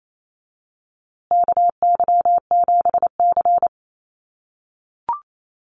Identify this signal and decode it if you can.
Morse code, heard as a keyed tone, about 28 wpm, 700 Hz. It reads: KY7C
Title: USA Call Signs Repeated 3x 28wpm